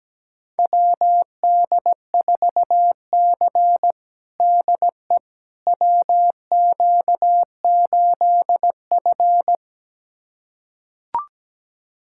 Morse code, heard as a keyed tone, about 17 wpm, 700 Hz. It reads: WD4C DE WQ8F
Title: USA Callsign DE Callsign - 4 Characters - 17wpm